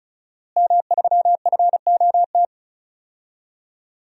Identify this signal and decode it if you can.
Morse code, heard as a keyed tone, about 34 wpm, 700 Hz. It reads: M3FOT
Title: DXCC-100 Call Signs - 5 Characters - Repeated 3x - 35wpm